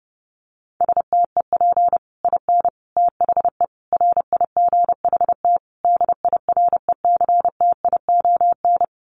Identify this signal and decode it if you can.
Morse code, heard as a keyed tone, about 30 wpm, 700 Hz. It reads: STEP IN THE RIGHT DIRECTION
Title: Sets of 5 Words Encore 30wpm